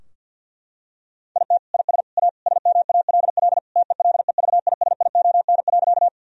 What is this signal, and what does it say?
Morse code, heard as a keyed tone, about 50 wpm, 700 Hz. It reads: IT IS A SMALL TELEVISION.
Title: Sentences from Top 1000 Words - Review Part 4 - 50wpm